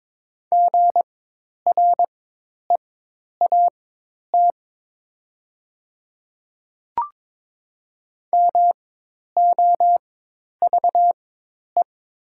Morse code, heard as a keyed tone, about 22 wpm, spaced as for 10 wpm, 700 Hz. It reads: GREAT MOVE
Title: 4x Character Spacing - Top 200 Words - 22wpm